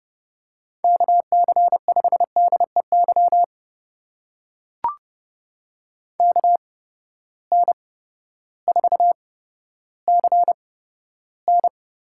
Morse code, heard as a keyed tone, about 30 wpm, spaced as for 14 wpm, 700 Hz. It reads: KC5DEY KN4CN
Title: USA Call Signs - 8x Character Spacing - 30wpm